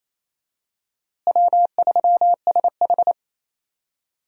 Morse code, heard as a keyed tone, about 27 wpm, 700 Hz. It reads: W3SH
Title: USA Call Signs - 4 Characters - Repeated 3x - 28wpm